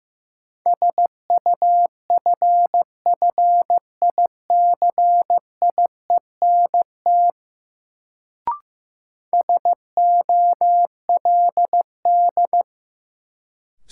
Morse code, heard as a keyed tone, about 15 wpm, 700 Hz. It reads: SUFFICIENT SOLD
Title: Sentences from Top 1400 Words - New Word Drill Alt - 15wpm